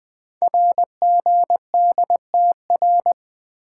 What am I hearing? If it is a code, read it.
Morse code, heard as a keyed tone, about 20 wpm, 700 Hz. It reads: RGDTR
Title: Code Groups - 5 Letters - 20wpm